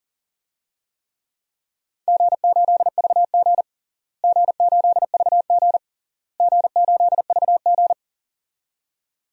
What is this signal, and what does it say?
Morse code, heard as a keyed tone, about 37 wpm, 700 Hz. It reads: G8VG G8VG G8VG
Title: DXCC-100 Call Signs - 4 Characters - Repeated 3x - 40wpm